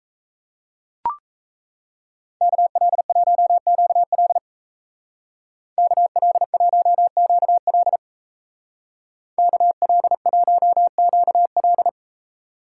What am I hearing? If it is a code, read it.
Morse code, heard as a keyed tone, about 37 wpm, 700 Hz. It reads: KL1QL KL1QL KL1QL